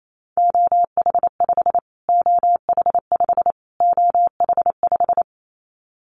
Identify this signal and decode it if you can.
Morse code, heard as a keyed tone, about 28 wpm, 700 Hz. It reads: OH5 OH5 OH5
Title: Callsign Prefixes for Top 25 DXCC - Repeated 3x - 28wpm